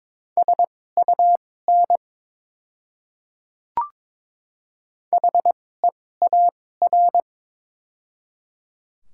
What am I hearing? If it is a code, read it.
Morse code, heard as a keyed tone, about 21 wpm, spaced as for 16 wpm, 700 Hz. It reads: SUN HEAR
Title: 2x Character Spacing - Top 300 Words - 22wpm